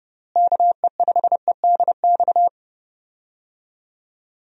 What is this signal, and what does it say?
Morse code, heard as a keyed tone, about 30 wpm, 700 Hz. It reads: KE5EDX
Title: USA Call Signs - 6 Characters - Repeated 2x - 30wpm